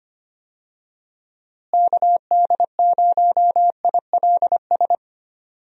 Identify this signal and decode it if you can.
Morse code, heard as a keyed tone, about 25 wpm, 700 Hz. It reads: KD0ILS
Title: USA Call Signs - 6 Characters - Repeated 3x - 25wpm